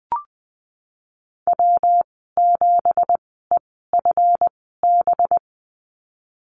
Morse code, heard as a keyed tone, about 20 wpm, spaced as for 14 wpm, 700 Hz. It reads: W7EFB
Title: 186 Lesson 30g - Letter b - Call Signs 20wpm